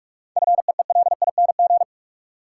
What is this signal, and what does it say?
Morse code, heard as a keyed tone, about 46 wpm, 700 Hz. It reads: FEELING
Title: Sentences from Top 1200 Words - New Word Drill Alt - 45wpm